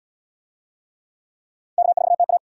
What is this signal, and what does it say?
Morse code, heard as a keyed tone, about 75 wpm, 700 Hz. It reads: K4TA